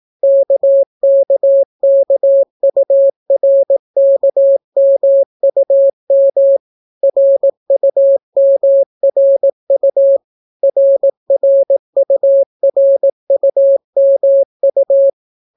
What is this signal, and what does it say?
Morse code, heard as a keyed tone, about 18 wpm, 550 Hz. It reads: KKKURKMUM RUMRU RRURUMU